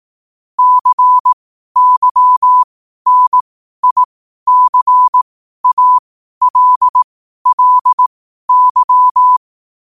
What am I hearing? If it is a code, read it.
Morse code, heard as a keyed tone, about 18 wpm, spaced as for 12 wpm, 1000 Hz. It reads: CYNICALLY